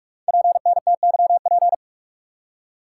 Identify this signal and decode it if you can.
Morse code, heard as a keyed tone, about 44 wpm, 700 Hz. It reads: PNTYP
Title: Code Groups - 5 Letter:Numbers - 45wpm